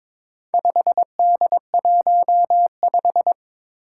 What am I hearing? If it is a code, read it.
Morse code, heard as a keyed tone, about 22 wpm, 700 Hz. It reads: 5D15
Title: Four Challenging Letter-Number Combinations 22wpm